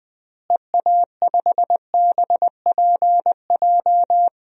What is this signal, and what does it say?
Morse code, heard as a keyed tone, about 21 wpm, 700 Hz. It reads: EA5BPJ
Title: DXCC-100 Call Signs - 6 Characters - Repeated 3x - 20wpm